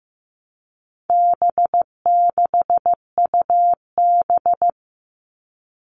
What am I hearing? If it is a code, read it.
Morse code, heard as a keyed tone, about 14 wpm, 700 Hz. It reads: B6UB
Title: All Cut Numbers - 4-Digit 15wpm